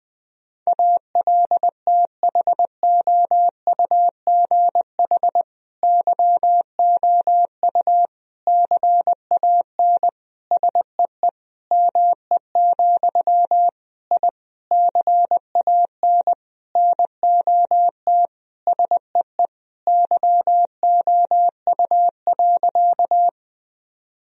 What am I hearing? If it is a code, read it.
Morse code, heard as a keyed tone, about 20 wpm, 700 Hz. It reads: ALTHOUGH YOU CAN SEE ME, I CAN NOT SEE YOU.
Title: Sentences from Top 1000 Words - Review Part 3 - 20wpm